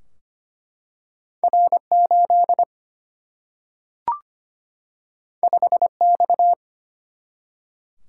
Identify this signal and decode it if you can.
Morse code, heard as a keyed tone, about 25 wpm, 700 Hz. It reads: R8 5X
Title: Single-Letter Single-Number 25wpm